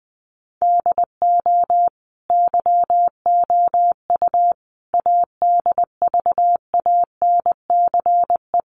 Morse code, heard as a keyed tone, about 20 wpm, 700 Hz. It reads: DO YOU ADVANCE
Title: Sentences from Top 1400 Words 20wpm